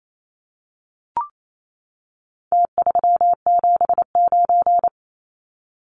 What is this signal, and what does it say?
Morse code, heard as a keyed tone, about 28 wpm, 700 Hz. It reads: T379
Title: Numbers - Contest Style - 4-Digit 28wpm